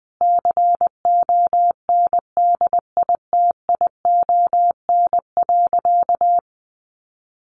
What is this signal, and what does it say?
Morse code, heard as a keyed tone, about 20 wpm, 700 Hz. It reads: CONDITION.
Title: Sentences from Top 1000 Words 20wpm